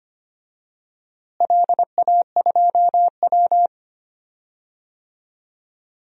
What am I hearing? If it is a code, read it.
Morse code, heard as a keyed tone, about 25 wpm, 700 Hz. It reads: LA2W
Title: DXCC-100 Call Signs - 4 Characters - Repeated 3x - 25wpm